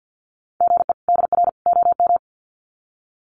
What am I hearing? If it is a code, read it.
Morse code, heard as a keyed tone, about 50 wpm, 700 Hz. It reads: GE DR OM
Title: Common QSO Phrases 50wpm